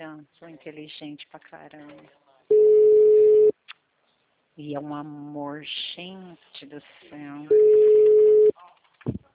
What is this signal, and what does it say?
Ringback tone (Europe, 425 Hz), with speech.